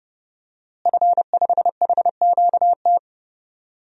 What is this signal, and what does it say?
Morse code, heard as a keyed tone, about 30 wpm, 700 Hz. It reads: F5HQT
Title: DXCC-100 Call Signs - 5 Characters - Repeated 2x - 30wpm